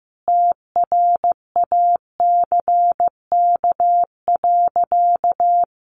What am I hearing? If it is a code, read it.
Morse code, heard as a keyed tone, about 15 wpm, 700 Hz. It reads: TRACK.
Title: Sentences from Top 1100 Words 15wpm